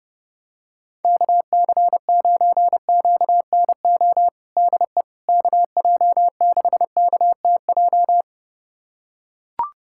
Morse code, heard as a keyed tone, about 32 wpm, 700 Hz. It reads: KC9QNO DE KJ6KTJ
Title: USA Callsign DE Callsign 30wpm